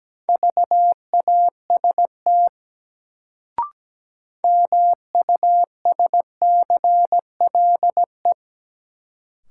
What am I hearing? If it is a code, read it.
Morse code, heard as a keyed tone, about 17 wpm, 700 Hz. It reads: VAST MUSCLE
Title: Sentences from Top 1300 Words - New Word Drill - 17wpm